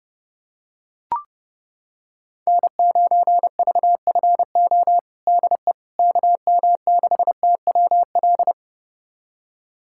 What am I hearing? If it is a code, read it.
Morse code, heard as a keyed tone, about 30 wpm, 700 Hz. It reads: N9VFO DE KM6TWL
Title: USA Callsign DE Callsign 30wpm